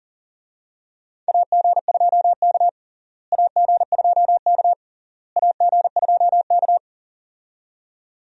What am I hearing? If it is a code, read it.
Morse code, heard as a keyed tone, about 37 wpm, 700 Hz. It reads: AG2K AG2K AG2K
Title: USA Call Signs - 4 Characters - Repeated 3x - 40wpm